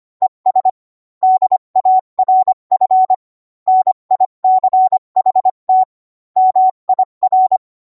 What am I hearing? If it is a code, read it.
Morse code, heard as a keyed tone, about 25 wpm, 750 Hz. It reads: ES DARF NICHT MIR